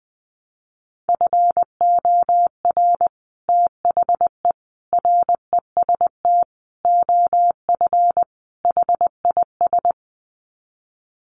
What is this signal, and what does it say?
Morse code, heard as a keyed tone, about 20 wpm, 700 Hz. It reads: FOR THE REST OF HIS
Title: Sets of 5 Words 20wpm